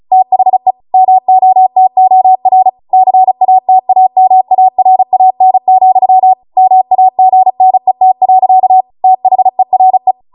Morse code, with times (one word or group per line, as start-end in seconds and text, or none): 0.1-0.7 THE
0.9-2.7 MOTOR
2.9-6.3 CATAMARAN,
6.6-8.8 MAGNET.
9.0-10.1 THERE